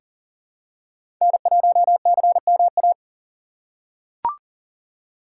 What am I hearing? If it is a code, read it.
Morse code, heard as a keyed tone, about 40 wpm, 700 Hz. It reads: N1CMA
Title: USA Call Signs - 3x Character Spacing - 40wpm